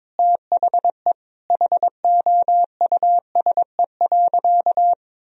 Morse code, heard as a keyed tone, about 22 wpm, 700 Hz. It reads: THE HOUSE.
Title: Sentences from Top 1000 Words - Review Part 1 - 22wpm